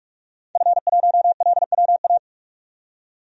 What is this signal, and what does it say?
Morse code, heard as a keyed tone, about 44 wpm, 700 Hz. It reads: F1LWA
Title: DXCC-100 Call Signs - 5 Characters - 45wpm